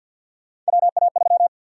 Morse code, heard as a keyed tone, about 52 wpm, 700 Hz. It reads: WA3
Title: USA Call Sign Prefixes - 50wpm